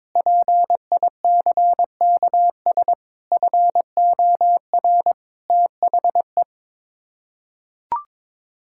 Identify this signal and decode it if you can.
Morse code, heard as a keyed tone, about 22 wpm, 700 Hz. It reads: PICKS FOR THE
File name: Sets of 9 Words - 22wpm